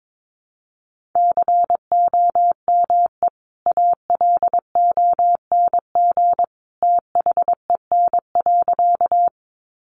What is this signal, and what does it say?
Morse code, heard as a keyed tone, about 22 wpm, 700 Hz. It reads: COME ALONG THEN.
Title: Sentences from Top 700 Words 22wpm